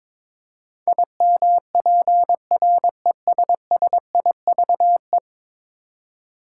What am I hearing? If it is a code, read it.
Morse code, heard as a keyed tone, about 21 wpm, 700 Hz. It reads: IMPRESSIVE
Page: morse-code-ninja-practice-22wpm